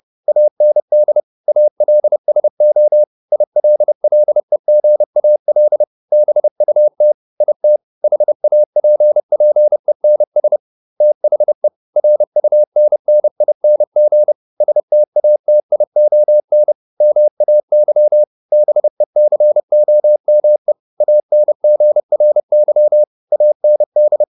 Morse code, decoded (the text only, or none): AND ALSO ILLEGAL BUT IT HAPPENS THE RUNNING STATION MAY BECOME ANGRY AND